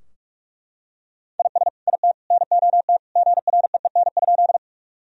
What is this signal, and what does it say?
Morse code, heard as a keyed tone, about 41 wpm, 700 Hz. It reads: IS IT NOT GREEN?
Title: Sentences from Top 1000 Words - Review Part 2 - 45wpm